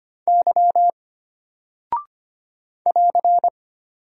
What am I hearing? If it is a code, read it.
Morse code, heard as a keyed tone, about 25 wpm, 700 Hz. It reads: Y <AR>
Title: Single Character (Letter:Number:Punctuation:Prosign) Alt 25wpm